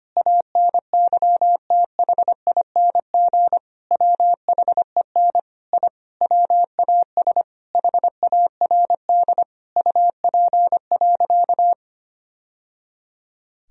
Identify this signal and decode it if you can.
Morse code, heard as a keyed tone, about 25 wpm, 700 Hz. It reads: ANYTHING WHEN I WAS HARD UP.
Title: Sentences from Top 1000 Words - Review Part 1 - 25wpm